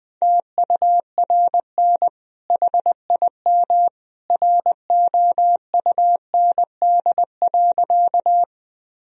Morse code, heard as a keyed tone, about 20 wpm, 700 Hz. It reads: TURN HIM ROUND.
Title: Sentences from Top 1000 Words - Review Part 1 - 20wpm